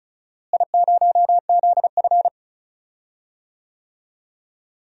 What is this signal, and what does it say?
Morse code, heard as a keyed tone, about 36 wpm, 700 Hz. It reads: I0ZF
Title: DXCC-100 Call Signs - 4 Characters - Repeated 2x - 35wpm